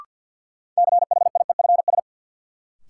Morse code, heard as a keyed tone, about 53 wpm, 700 Hz. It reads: CHIEFS
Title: Sets of 5 Words Encore - New Word Drill - 50wpm